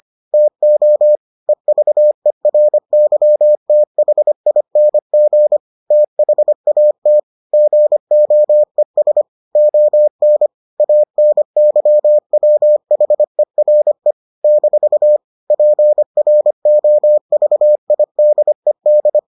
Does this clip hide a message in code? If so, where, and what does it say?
Yes: Morse code from 0.3 to 19.2 s, heard as a keyed tone, about 25 wpm, 600 Hz. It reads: TO EVERYTHING THAT GOES ON ANYWHERE - PROVIDED